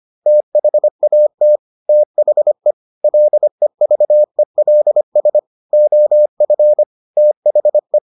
Morse code, heard as a keyed tone, about 25 wpm, 600 Hz. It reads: THAT THE LEVELS OF THE